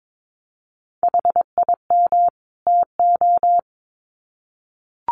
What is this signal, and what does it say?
Morse code, heard as a keyed tone, about 22 wpm, 700 Hz. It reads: HIM TO
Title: Sets of 2 Words 22wpm